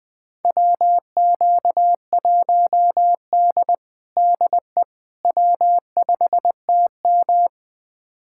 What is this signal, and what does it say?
Morse code, heard as a keyed tone, about 20 wpm, 700 Hz. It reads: WQ1D DE W5TM
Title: USA Callsign DE Callsign - 4 Characters - 20wpm